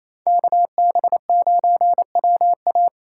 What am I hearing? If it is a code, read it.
Morse code, heard as a keyed tone, about 35 wpm, 700 Hz. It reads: KB9WA
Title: Speed Racing - USA 2x2 Call Signs 28wpm